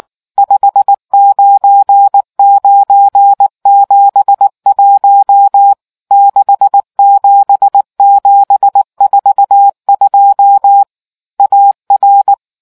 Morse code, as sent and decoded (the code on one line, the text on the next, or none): ..... ----. ----. --... .---- / -.... --... --... ....- ..--- / .- .-.
59971 67742 AR